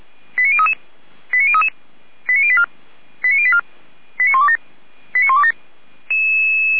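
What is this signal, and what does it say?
Five-tone selective call (ZVEI), heard as quick runs of five tones, tones of about 70 ms. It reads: 89033 89033 89084 89084 89127 89127
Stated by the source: Received by SDR, archived at audio.f85360.feuerwehr-orlamuende.org (fire brigade Orlamünde).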